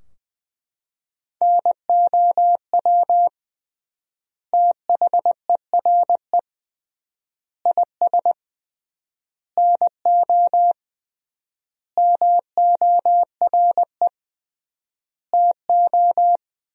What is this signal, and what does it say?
Morse code, heard as a keyed tone, about 20 wpm, 700 Hz. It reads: NOW THERE IS NO MORE TO
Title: 3x Word-Spacing - Sentences from Top 100 Words - 20wpm